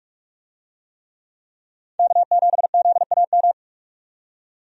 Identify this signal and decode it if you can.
Morse code, heard as a keyed tone, about 46 wpm, 700 Hz. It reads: K7ZAM
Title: USA Call Signs - 5 Characters - 45wpm